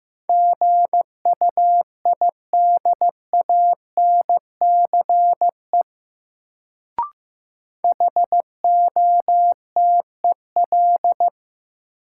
Morse code, heard as a keyed tone, about 15 wpm, 700 Hz. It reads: GUIDANCE HOTEL
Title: Sentences from Top 1600 Words - New Word Drill Alt - 15wpm